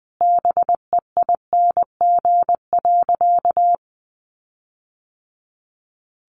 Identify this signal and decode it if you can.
Morse code, heard as a keyed tone, about 20 wpm, 700 Hz. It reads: BEING.
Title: Sentences from Top 1500 Words 20wpm